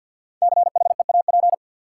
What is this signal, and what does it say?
Morse code, heard as a keyed tone, about 50 wpm, 700 Hz. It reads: CHEAP
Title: Sentences from Top 1200 Words - New Word Drill Alt - 50wpm